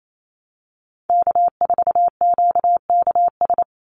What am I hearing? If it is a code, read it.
Morse code, heard as a keyed tone, about 28 wpm, 700 Hz. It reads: K4QKS
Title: USA Call Signs 28wpm